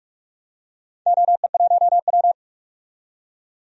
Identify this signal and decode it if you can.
Morse code, heard as a keyed tone, about 43 wpm, 700 Hz. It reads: OE1W
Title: DXCC-100 Call Signs - 4 Characters - 45wpm